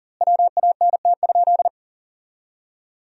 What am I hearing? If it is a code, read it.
Morse code, heard as a keyed tone, about 40 wpm, 700 Hz. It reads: WANT?
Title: 2x Word-Spacing - Sentences from Top 200 Words - 40wpm